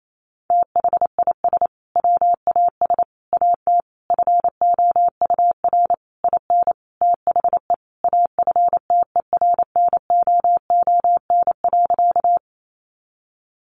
Morse code, heard as a keyed tone, about 28 wpm, 700 Hz. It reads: THIS WAS AT FOUR IN THE AFTERNOON.
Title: Sentences from Top 1500 Words 28wpm